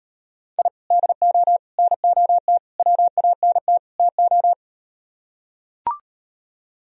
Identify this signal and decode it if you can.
Morse code, heard as a keyed tone, about 38 wpm, 700 Hz. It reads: I DO NOT WANT TO